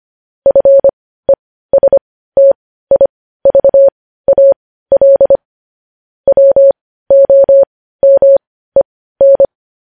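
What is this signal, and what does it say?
Morse code, heard as a keyed tone, about 25 wpm, spaced as for 15 wpm, 550 Hz. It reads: FESTIVAL WOMEN
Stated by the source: Morse code news headlines sent at 15.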